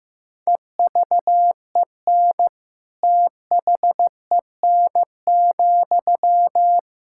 Morse code, heard as a keyed tone, about 15 wpm, 700 Hz. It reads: EVEN THEN,
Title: Sentences from Top 1600 Words 15wpm